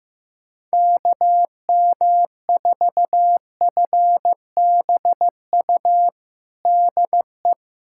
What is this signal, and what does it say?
Morse code, heard as a keyed tone, about 15 wpm, 700 Hz. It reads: KM4FBU DE
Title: USA Callsign DE Callsign 15wpm